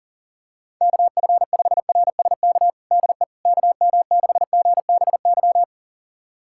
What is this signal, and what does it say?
Morse code, heard as a keyed tone, about 40 wpm, 700 Hz. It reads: KF5RSK DE KM6GBY
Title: USA Callsign DE Callsign - 6 Characters - 40wpm